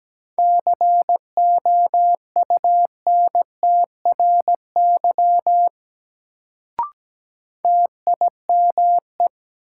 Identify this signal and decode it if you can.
Morse code, heard as a keyed tone, about 17 wpm, 700 Hz. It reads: COUNTRY TIME
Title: Sets of 2 Words - New Word Drill - 17wpm